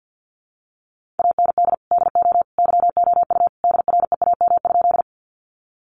Morse code, heard as a keyed tone, about 50 wpm, 700 Hz. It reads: AND DO YOU DREAM?
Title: Sentences from Top 1500 Words 50wpm